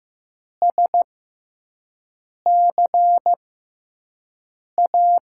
Morse code, heard as a keyed tone, about 15 wpm, spaced as for 5 wpm, 700 Hz. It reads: SCA
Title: USA Call Signs - 6 Characters - 6x Character Spacing - 15wpm